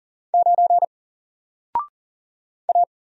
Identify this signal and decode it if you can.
Morse code, heard as a keyed tone, about 40 wpm, 700 Hz. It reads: 9 A
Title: Single Letter-Number Alt 40wpm